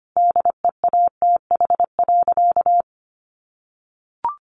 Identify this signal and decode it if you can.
Morse code, heard as a keyed tone, about 25 wpm, 700 Hz. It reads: DEATH.
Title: Sentences from Top 1100 Words 25wpm